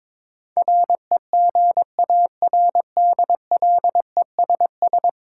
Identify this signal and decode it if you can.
Morse code, heard as a keyed tone, about 22 wpm, 700 Hz. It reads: REGARDLESS
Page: morse-code-ninja-practice-22wpm